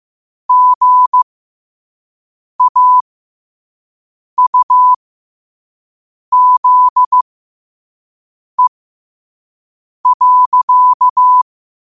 Morse code, heard as a keyed tone, about 15 wpm, spaced as for 5 wpm, 1000 Hz. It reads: GAUZE.
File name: 5wpm